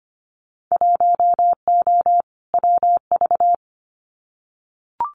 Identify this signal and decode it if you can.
Morse code, heard as a keyed tone, about 25 wpm, 700 Hz. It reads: 1O WV
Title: Winter Field Day Exchanges 25wpm